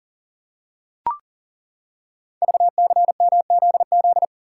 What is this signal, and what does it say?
Morse code, heard as a keyed tone, about 37 wpm, 700 Hz. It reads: VCMZZ
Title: Code Groups - 5 Letters - 40wpm